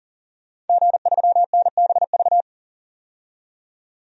Morse code, heard as a keyed tone, about 41 wpm, 700 Hz. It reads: G3NBV
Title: DXCC-100 Call Signs - 5 Characters - 40wpm